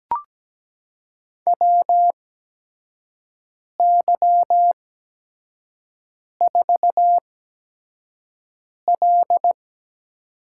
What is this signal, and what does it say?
Morse code, heard as a keyed tone, about 17 wpm, spaced as for 5 wpm, 700 Hz. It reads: WY4L
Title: USA Call Signs - 4 Characters - 8x Character Spacing - 17wpm